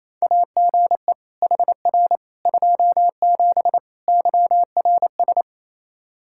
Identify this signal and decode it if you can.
Morse code, heard as a keyed tone, about 28 wpm, 700 Hz. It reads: AGE HR 27 YRS
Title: Practice QSO - Exchange 3 - More Info - 28wpm